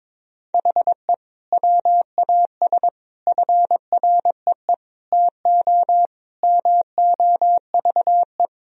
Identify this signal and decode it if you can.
Morse code, heard as a keyed tone, about 21 wpm, 700 Hz. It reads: HE WAS FREE TO MOVE
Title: Sentences from Top 1000 Words - Review Part 1 - 22wpm